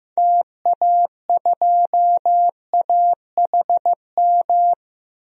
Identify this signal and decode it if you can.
Morse code, heard as a keyed tone, about 15 wpm, 700 Hz. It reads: TA2AHM
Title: DXCC-100 Call Signs - 6 Characters - 15wpm